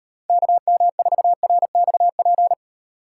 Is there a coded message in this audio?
Yes: Morse code from 0.3 to 2.5 s, heard as a keyed tone, about 38 wpm, 700 Hz. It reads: KM4RXP